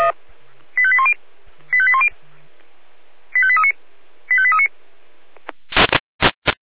Five-tone selective call (ZVEI), heard as quick runs of five tones, tones of about 70 ms. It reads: 85810 85810 85839 85839